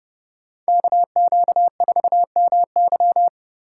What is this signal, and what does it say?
Morse code, heard as a keyed tone, about 30 wpm, 700 Hz. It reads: KQ4MY